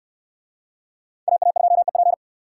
Morse code, heard as a keyed tone, about 69 wpm, 700 Hz. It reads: AA2EP